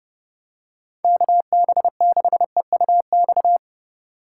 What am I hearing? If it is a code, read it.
Morse code, heard as a keyed tone, about 30 wpm, 700 Hz. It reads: KB6EUX